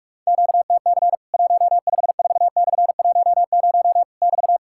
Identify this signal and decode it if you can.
Morse code, heard as a keyed tone, about 46 wpm, 700 Hz. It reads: QTC 154/10 <BT>
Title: Worked All Europe DX Contest - QTC Exchange - 45wpm